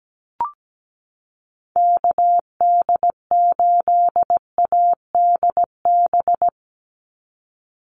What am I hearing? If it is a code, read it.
Morse code, heard as a keyed tone, about 17 wpm, 700 Hz. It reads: KD8ADB
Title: USA Call Signs 17wpm